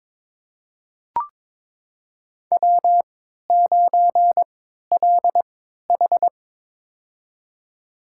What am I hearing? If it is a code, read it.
Morse code, heard as a keyed tone, about 21 wpm, spaced as for 12 wpm, 700 Hz. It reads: W9LH
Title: USA Call Signs - 4 Characters - 3x Character Spacing - 22wpm